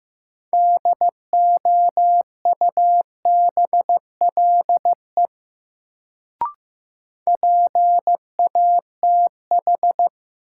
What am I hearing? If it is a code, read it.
Morse code, heard as a keyed tone, about 15 wpm, 700 Hz. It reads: DOUBLE PATH
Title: Sentences from Top 1400 Words - New Word Drill Alt - 15wpm